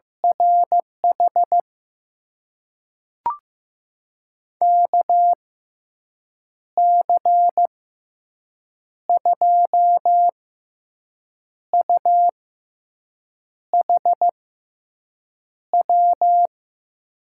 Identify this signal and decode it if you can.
Morse code, heard as a keyed tone, about 15 wpm, spaced as for 6 wpm, 700 Hz. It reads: RH KC2UHW
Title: USA Call Signs - 6 Characters - 6x Character Spacing - 15wpm